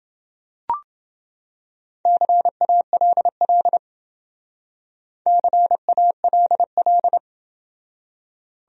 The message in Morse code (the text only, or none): CALL CALL